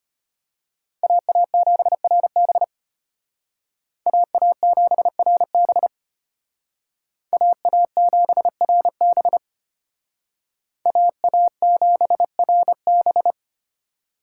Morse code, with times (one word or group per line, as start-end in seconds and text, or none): 1.0-2.6 AA7RB
4.1-5.9 AA7RB
7.3-9.4 AA7RB
10.9-13.3 AA7RB